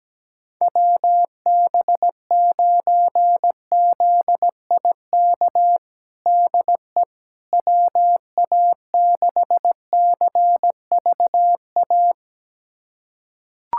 Morse code, heard as a keyed tone, about 17 wpm, 700 Hz. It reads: WB9ZIK DE WA6CVA
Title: USA Callsign DE Callsign - 6 Characters - 17wpm